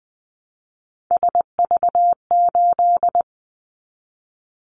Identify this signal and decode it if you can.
Morse code, heard as a keyed tone, about 20 wpm, 700 Hz. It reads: SV8
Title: Callsign Prefixes for Top 50 DXCC - 20wpm